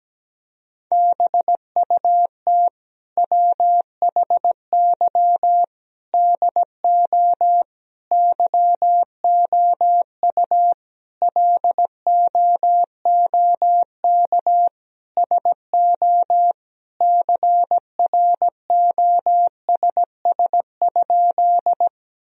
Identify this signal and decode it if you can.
Morse code, heard as a keyed tone, about 17 wpm, 700 Hz. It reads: BUT WHY DO YOU LOOK SO CROSS?
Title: Sentences from Top 1000 Words - Review Part 1 - 17wpm